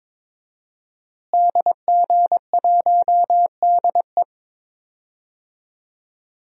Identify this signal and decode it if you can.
Morse code, heard as a keyed tone, about 21 wpm, 700 Hz. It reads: DG1DE
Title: DXCC-100 Call Signs - 5 Characters - Repeated 3x - 22wpm